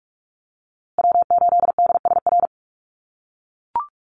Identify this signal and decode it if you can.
Morse code, heard as a keyed tone, about 44 wpm, 700 Hz. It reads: W8DSR
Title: USA Call Signs Repeated 3x 45wpm